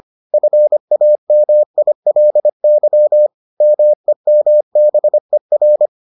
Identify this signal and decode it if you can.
Morse code, heard as a keyed tone, about 25 wpm, 600 Hz. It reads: FAMILY MEMBER